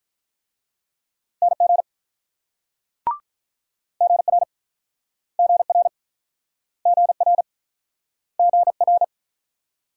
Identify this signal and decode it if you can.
Morse code, heard as a keyed tone, about 48 wpm, 700 Hz. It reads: NG GR GR GR GR